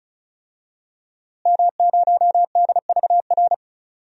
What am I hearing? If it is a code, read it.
Morse code, heard as a keyed tone, about 34 wpm, 700 Hz. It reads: M0DVR
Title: DXCC-100 Call Signs - 5 Characters - Repeated 2x - 35wpm